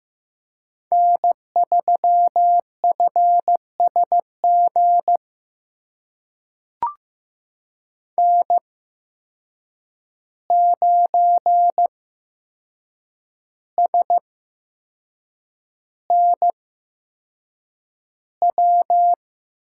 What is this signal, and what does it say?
Morse code, heard as a keyed tone, about 15 wpm, spaced as for 6 wpm, 700 Hz. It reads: N3FSG N9SNW
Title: USA Call Signs - 5 Characters - 8x Character Spacing - 15wpm